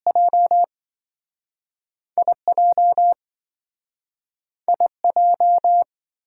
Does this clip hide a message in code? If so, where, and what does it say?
Yes: Morse code from 0.1 to 5.8 s, heard as a keyed tone, about 27 wpm, spaced as for 5 wpm, 700 Hz. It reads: JI JI J